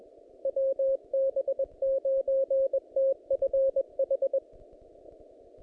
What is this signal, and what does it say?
Morse code, heard as a keyed tone, about 20 wpm, 540 Hz. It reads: WB9TFH
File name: contests / naqp-cw-k5tr-op-k5pi- -2200